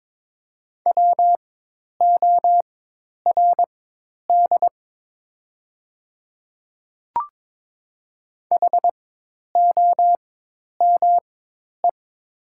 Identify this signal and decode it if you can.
Morse code, heard as a keyed tone, about 22 wpm, spaced as for 10 wpm, 690 Hz. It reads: WORD HOME
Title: 4x Character Spacing - Top 200 Words - 22wpm